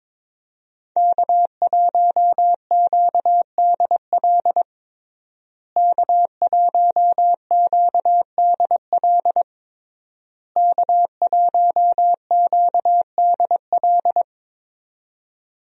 Morse code, heard as a keyed tone, about 22 wpm, 700 Hz. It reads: K1QDL K1QDL K1QDL
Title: USA Call Signs - 5 Characters - Repeated 3x - 22wpm